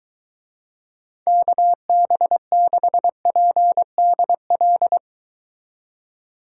Morse code, heard as a keyed tone, about 23 wpm, 700 Hz. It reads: KB6PDL